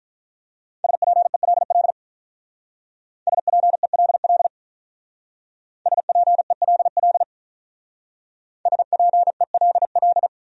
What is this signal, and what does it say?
Morse code, heard as a keyed tone, about 45 wpm, 700 Hz. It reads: SPELL SPELL SPELL SPELL